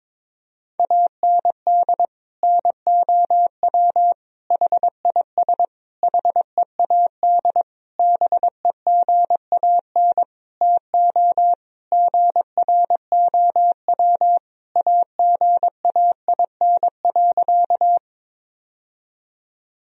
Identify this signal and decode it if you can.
Morse code, heard as a keyed tone, about 22 wpm, 700 Hz. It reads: AND NOW HIS HEAD BEGAN TO GROW AGAIN.
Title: Sentences from Top 1000 Words - Review Part 1 - 22wpm